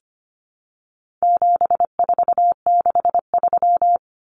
Morse code, heard as a keyed tone, about 25 wpm, 700 Hz. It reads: 7463
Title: Common Cut Numbers - 4-Digit 25wpm